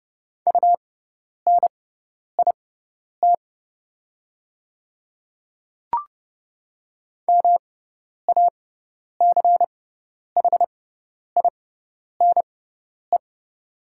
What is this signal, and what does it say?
Morse code, heard as a keyed tone, about 30 wpm, spaced as for 10 wpm, 700 Hz. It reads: UNIT MACHINE
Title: 6x Character Spacing - Top 500 Words - 30wpm